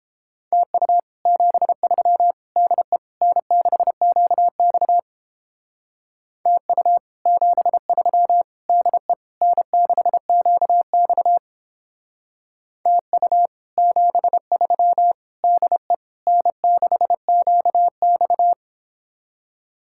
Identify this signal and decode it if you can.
Morse code, heard as a keyed tone, about 31 wpm, 700 Hz. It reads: TU 73 DE N6QX TU 73 DE N6QX TU 73 DE N6QX